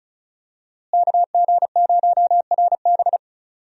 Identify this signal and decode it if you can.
Morse code, heard as a keyed tone, about 36 wpm, 700 Hz. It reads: KG0RB
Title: USA Call Signs - 5 Characters - Repeated 3x - 35wpm